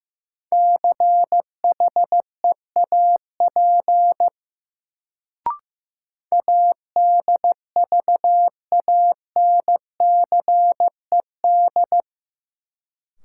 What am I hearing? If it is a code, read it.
Morse code, heard as a keyed tone, about 16 wpm, 700 Hz. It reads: CHEAP ADVANCED
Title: Sentences from Top 1200 Words - New Word Drill - 15wpm